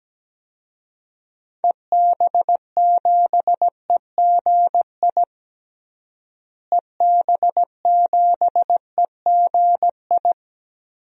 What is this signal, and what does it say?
Morse code, heard as a keyed tone, about 17 wpm, 700 Hz. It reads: EB7EGI EB7EGI
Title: DXCC-100 Call Signs - 6 Characters - Repeated 3x - 17wpm